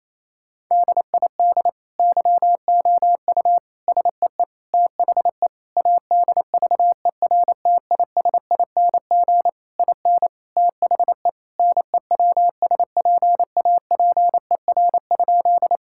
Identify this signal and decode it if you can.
Morse code, heard as a keyed tone, about 27 wpm, 700 Hz. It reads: DID YOU SEE THE ADVERTISING IN THE NEWSPAPER?
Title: Sentences from Top 1000 Words - Review Part 6 - 28wpm